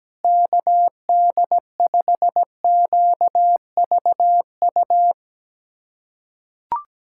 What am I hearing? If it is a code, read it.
Morse code, heard as a keyed tone, about 17 wpm, 700 Hz. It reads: KD5QVU
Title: USA Callsign DE Callsign - 6 Characters - 17wpm